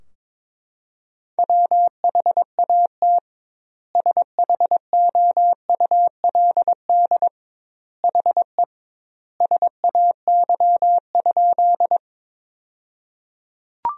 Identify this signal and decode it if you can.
Morse code, heard as a keyed tone, about 22 wpm, 700 Hz. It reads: WHAT SHOULD HE SAY?
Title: 2x Word-Spacing - Sentences from Top 300 Words - 22wpm